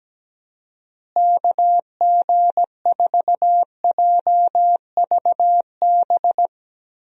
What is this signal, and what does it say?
Morse code, heard as a keyed tone, about 17 wpm, 700 Hz. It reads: KG4JVB
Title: USA Call Signs - 6 Characters - Repeated 2x - 17wpm